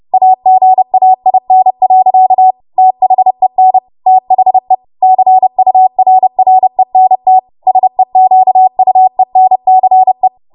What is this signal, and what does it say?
Morse code, heard as a keyed tone, about 30 wpm, 750 Hz. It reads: AGAIN. THEN THE CURRENT SEQUENCE
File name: Morse / 30WPM